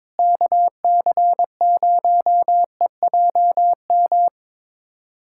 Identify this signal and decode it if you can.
Morse code, heard as a keyed tone, about 21 wpm, 700 Hz. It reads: KC0EJM
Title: USA Callsign DE Callsign 22wpm